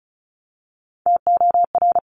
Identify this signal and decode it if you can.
Morse code, heard as a keyed tone, about 36 wpm, 700 Hz. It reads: TOR
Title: Top Three Letter Combinations in Words 35wpm